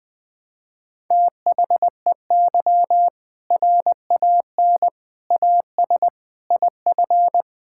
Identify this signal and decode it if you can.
Morse code, heard as a keyed tone, about 20 wpm, 700 Hz. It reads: THEY RAN AS IF
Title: Sentences with 4-Word Phrases - Spoken Beginning - 20wpm